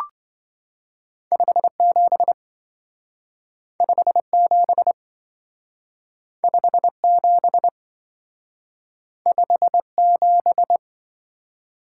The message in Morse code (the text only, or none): 57 57 57 57